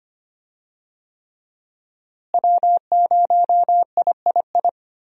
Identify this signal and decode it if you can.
Morse code, heard as a keyed tone, about 25 wpm, 700 Hz. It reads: W0III
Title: USA Call Signs - 5 Characters - Repeated 3x - 25wpm